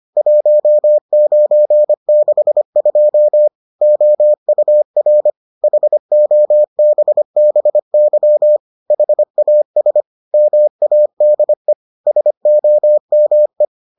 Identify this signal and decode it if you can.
Morse code, heard as a keyed tone, about 25 wpm, 600 Hz. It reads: 1962 OUR HOBBY HAS MADE SOME